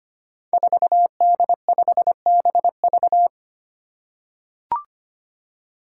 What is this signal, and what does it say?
Morse code, heard as a keyed tone, about 25 wpm, 700 Hz. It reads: 4D5BV
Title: Five Challenging Letter-Number Combinations 25wpm